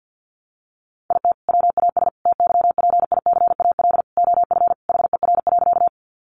Morse code, heard as a keyed tone, about 50 wpm, 700 Hz. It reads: IT WAS TYPICAL OF HER.
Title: Sentences from Top 1100 Words 50wpm